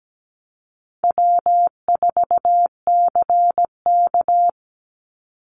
Morse code, heard as a keyed tone, about 17 wpm, 700 Hz. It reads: W4CK
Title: Call Signs - 17wpm